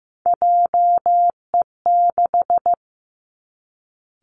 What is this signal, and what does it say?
Morse code, heard as a keyed tone, about 15 wpm, 700 Hz. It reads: JE6
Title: Three Challenging Letter-Number Combinations 15wpm